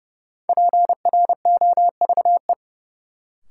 Morse code, heard as a keyed tone, about 30 wpm, 700 Hz. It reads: PROVE
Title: Sentences from Top 1600 Words - New Word Drill - 30wpm